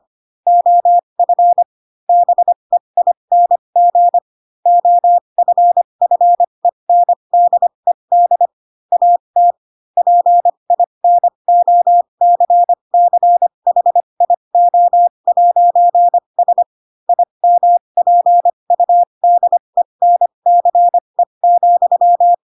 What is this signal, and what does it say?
Morse code, heard as a keyed tone, about 25 wpm, spaced as for 22 wpm, 700 Hz. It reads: OF BEING OFFENDED AT PINOCCHIO'S IMPUDENCE,
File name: Pinocchio / Pinocchio ch4 18 25 ik1hge